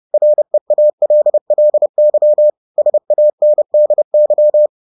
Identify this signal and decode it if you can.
Morse code, heard as a keyed tone, about 30 wpm, 600 Hz. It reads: REALLY SANDY
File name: 30wpm-7